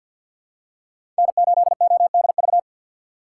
Morse code, heard as a keyed tone, about 53 wpm, 700 Hz. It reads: N8ODV
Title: USA Call Signs - 5 Characters - 3x Character Spacing - 50wpm